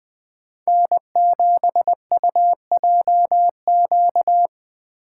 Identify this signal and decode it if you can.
Morse code, heard as a keyed tone, about 20 wpm, 700 Hz. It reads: N7UJQ
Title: USA Call Signs - 3x Character Spacing - 20wpm